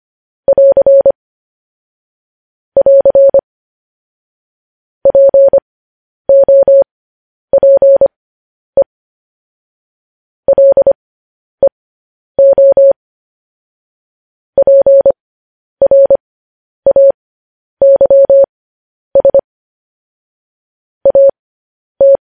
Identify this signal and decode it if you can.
Morse code, heard as a keyed tone, about 25 wpm, spaced as for 10 wpm, 550 Hz. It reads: <AR> <AR> POPE LEO PRAYS AT
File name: podcast / News headlines-1x-25x10